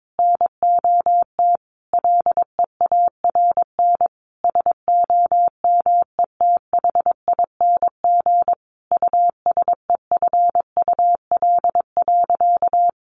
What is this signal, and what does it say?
Morse code, heard as a keyed tone, about 22 wpm, 700 Hz. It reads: NOT LEARN SOMETHING USEFUL.
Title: Sentences from Top 1400 Words 22wpm